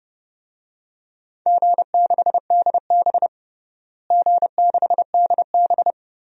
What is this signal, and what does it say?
Morse code, heard as a keyed tone, about 30 wpm, 700 Hz. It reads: G6DB G6DB
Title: DXCC-100 Call Signs - 4 Characters - Repeated 2x - 30wpm